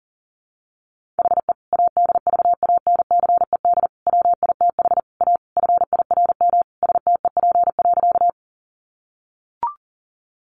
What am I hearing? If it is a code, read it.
Morse code, heard as a keyed tone, about 40 wpm, 700 Hz. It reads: HE ADVANCED WITH A FIRM STEP.
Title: Sentences from Top 1400 Words 40wpm